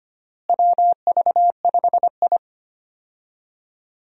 Morse code, heard as a keyed tone, about 25 wpm, 700 Hz. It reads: WV5I
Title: USA Call Signs - 4 Characters - Repeated 2x - 25wpm